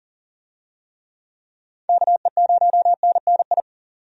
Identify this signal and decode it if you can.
Morse code, heard as a keyed tone, about 40 wpm, 700 Hz. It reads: KE0NNI
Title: USA Call Signs - 6 Characters - 40wpm